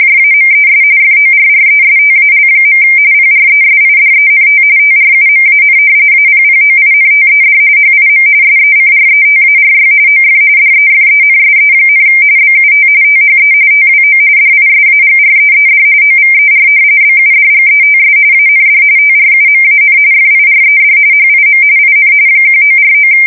RTTY, about 45.45 baud. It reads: JAMES FORRESTAL. PRESIDENT TRUMAN APPOINTED HIM SECRETARY OF DEFENSE, BUT THE TWO DISAGREED OVER BUDGETS AND TRUMAN FIRED HIM IN 1949
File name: fsk170 / usnavy5